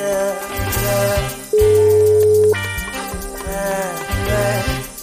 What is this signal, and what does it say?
Ringback tone (Europe, 425 Hz), with noise.